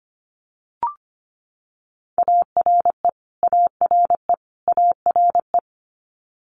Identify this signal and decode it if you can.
Morse code, heard as a keyed tone, about 25 wpm, 700 Hz. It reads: ARE ARE ARE
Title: Multi-letter Phonemes Repeated 3x 25wpm